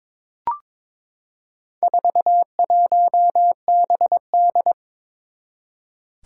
Morse code, heard as a keyed tone, about 22 wpm, 700 Hz. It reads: 41BD
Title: Four Challenging Letter-Number Combinations 22wpm